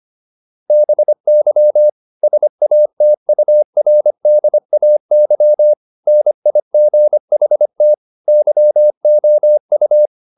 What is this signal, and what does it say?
Morse code, heard as a keyed tone, about 25 wpm, 600 Hz. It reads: BY SATURDAY NIGHT YOU